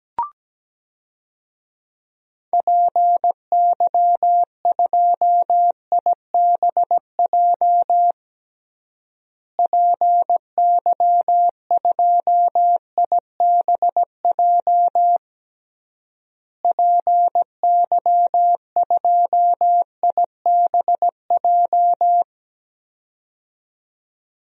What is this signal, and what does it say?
Morse code, heard as a keyed tone, about 17 wpm, 700 Hz. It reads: PY2IBJ PY2IBJ PY2IBJ
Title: DXCC-100 Call Signs - 6 Characters - Repeated 3x - 17wpm